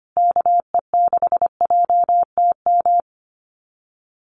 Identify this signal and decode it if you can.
Morse code, heard as a keyed tone, about 25 wpm, 700 Hz. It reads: KE6JTM
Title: USA Call Signs Repeated 3x 25wpm